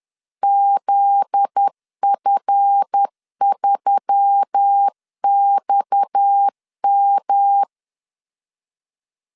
Morse code, heard as a keyed tone, about 10 wpm, 800 Hz. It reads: ZF3XM
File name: LEZIONI / 7 GF9Z / 7 GF9Z 10wpm 2611